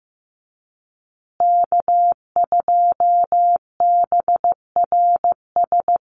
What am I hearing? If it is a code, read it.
Morse code, heard as a keyed tone, about 15 wpm, 700 Hz. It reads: K2BRS
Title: USA Call Signs Repeated 2x 15wpm